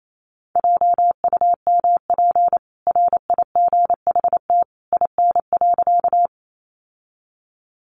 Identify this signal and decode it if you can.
Morse code, heard as a keyed tone, about 28 wpm, 700 Hz. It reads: JUMP RIGHT IN.
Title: Sentences from Top 1300 Words 28wpm